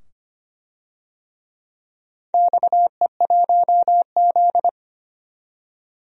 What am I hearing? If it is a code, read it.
Morse code, heard as a keyed tone, about 25 wpm, 700 Hz. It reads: XE1Z
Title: DXCC-100 Call Signs - 4 Characters - 25wpm